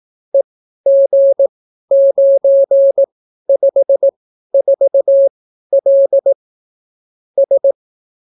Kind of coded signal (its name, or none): Morse code